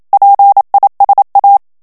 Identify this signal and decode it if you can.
Morse code, heard as a keyed tone, about 27 wpm, 800 Hz. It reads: PISA
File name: CW / FICHIERS-MP3 / FRANCAIS TEXTE CLAIR 1 VITESSE 26 MOTS MINUTES